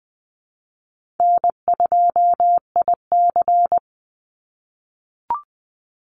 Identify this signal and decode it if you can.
Morse code, heard as a keyed tone, about 20 wpm, 700 Hz. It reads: N2IC
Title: Easy Call Signs - 20wpm